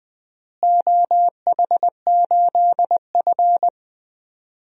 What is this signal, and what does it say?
Morse code, heard as a keyed tone, about 20 wpm, 700 Hz. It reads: OH8F
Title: DXCC-100 Call Signs - 4 Characters - Repeated 3x - 20wpm